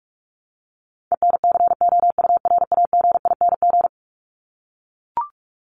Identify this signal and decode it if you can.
Morse code, heard as a keyed tone, about 44 wpm, 700 Hz. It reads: ENCOURAGING
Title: Top 1751-2000 Words 45wpm